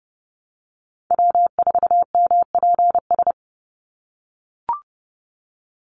Morse code, heard as a keyed tone, about 30 wpm, 700 Hz. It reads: W4MPS
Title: NA SOTA Chaser Callsigns 30wpm